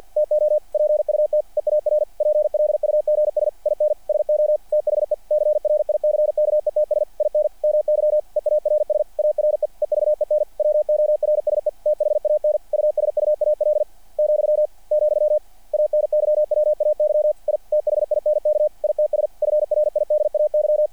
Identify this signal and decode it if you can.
Morse code, heard as a keyed tone, about 48 wpm, 600 Hz. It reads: TO PUT EAR PLUGS IN SO THE CRICKETS IN MY EARS ARE EVEN WORSE THAN USUAL , , ANYWAY I THINK ITS FRIDAY
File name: archived / CRSnet